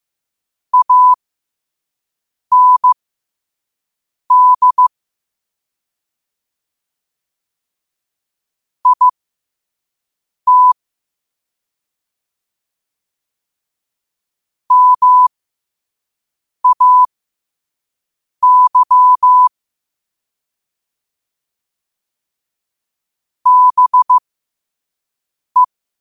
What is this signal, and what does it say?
Morse code, heard as a keyed tone, about 15 wpm, spaced as for 5 wpm, 1000 Hz. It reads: AND IT MAY BE
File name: qotdfiles / 5wpm